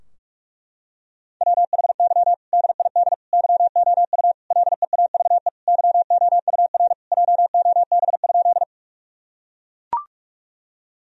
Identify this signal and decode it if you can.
Morse code, heard as a keyed tone, about 41 wpm, 700 Hz. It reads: WHY DID YOU LEAVE YOUR JOB?
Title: Sentences from Top 1000 Words - Review Part 3 - 45wpm